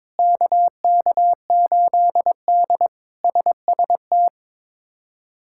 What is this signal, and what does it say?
Morse code, heard as a keyed tone, about 22 wpm, 700 Hz. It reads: KK8D SST
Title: K1USN SST QSO Practice - 22wpm